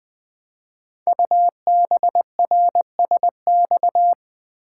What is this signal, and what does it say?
Morse code, heard as a keyed tone, about 20 wpm, 700 Hz. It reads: UBRSX
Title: Code Groups - 5 Letters - 20wpm